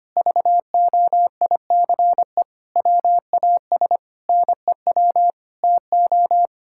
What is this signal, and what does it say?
Morse code, heard as a keyed tone, about 25 wpm, 700 Hz. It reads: VOICE WAS NEW TO
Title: Sentences from Top 1600 Words 25wpm